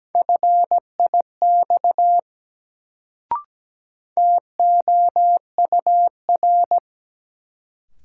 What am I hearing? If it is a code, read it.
Morse code, heard as a keyed tone, about 17 wpm, 700 Hz. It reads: FIX TOUR
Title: Sentences from Top 1400 Words - New Word Drill - 17wpm